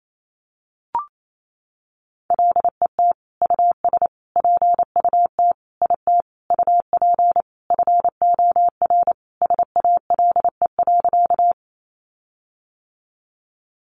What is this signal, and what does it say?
Morse code, heard as a keyed tone, about 28 wpm, 700 Hz. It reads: LET US PUT IT UP FOR SALE.
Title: Sentences from Top 1400 Words 28wpm